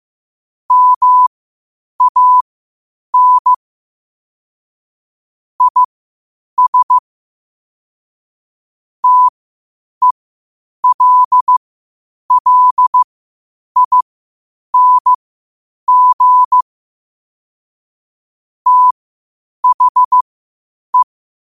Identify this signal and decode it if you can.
Morse code, heard as a keyed tone, about 15 wpm, spaced as for 8 wpm, 1000 Hz. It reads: MAN IS TELLING THE
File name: qotdfiles / 8wpm